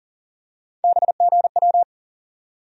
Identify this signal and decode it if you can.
Morse code, heard as a keyed tone, about 40 wpm, 700 Hz. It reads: BGW
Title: USA Call Sign Postfixes - 3 Letters 40wpm